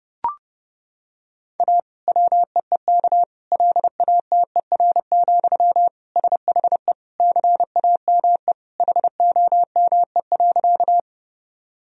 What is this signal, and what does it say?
Morse code, heard as a keyed tone, about 32 wpm, 700 Hz. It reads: A WEEK LATER, SHE CAME HOME.
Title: Sentences from Top 1000 Words - Review Part 4 - 30wpm